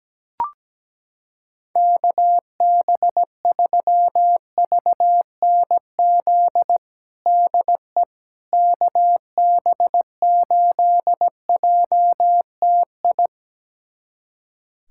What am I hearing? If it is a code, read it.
Morse code, heard as a keyed tone, about 17 wpm, 700 Hz. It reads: KB3VNZ DE KB8JTI
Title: USA Callsign DE Callsign 17wpm